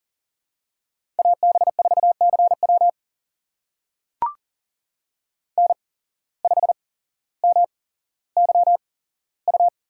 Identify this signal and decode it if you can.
Morse code, heard as a keyed tone, about 39 wpm, spaced as for 17 wpm, 700 Hz. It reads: AB4CW N5MYU
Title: USA Call Signs - 8x Character Spacing - 40wpm